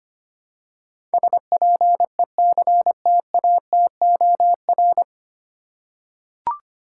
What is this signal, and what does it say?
Morse code, heard as a keyed tone, about 25 wpm, 700 Hz. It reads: SPECTATOR
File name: Nine Letter Words - 25wpm